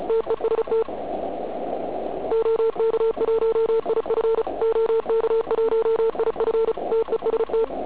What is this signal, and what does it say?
Morse code, heard as a keyed tone, about 34 wpm, 440 Hz, with noise. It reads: TEST OK1IF OK1IF TEST
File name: Archiv / CUC106 / cuc106 PILEUP